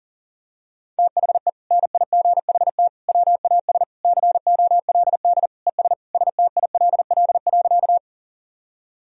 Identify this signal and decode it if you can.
Morse code, heard as a keyed tone, about 40 wpm, 700 Hz. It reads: THE NIGHT WAS COLD ES STILL.
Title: Sentences from Top 400 Words and Abbreviations 40wpm